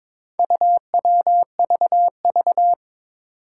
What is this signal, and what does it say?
Morse code, heard as a keyed tone, about 22 wpm, 700 Hz. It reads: UWVV
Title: Four Challenging Letter-Number Combinations 22wpm